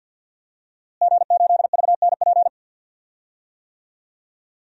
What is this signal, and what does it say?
Morse code, heard as a keyed tone, about 52 wpm, 700 Hz. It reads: G8VNP
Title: DXCC-100 Call Signs - 5 Characters - Repeated 3x - 50wpm